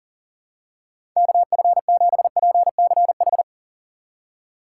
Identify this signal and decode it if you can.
Morse code, heard as a keyed tone, about 40 wpm, 700 Hz. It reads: KF7PCH
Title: USA Call Signs - 6 Characters - 3x Character Spacing - 40wpm